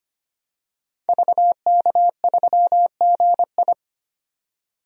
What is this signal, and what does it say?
Morse code, heard as a keyed tone, about 25 wpm, 700 Hz. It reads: VK3GI
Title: DXCC-100 Call Signs - 5 Characters - 25wpm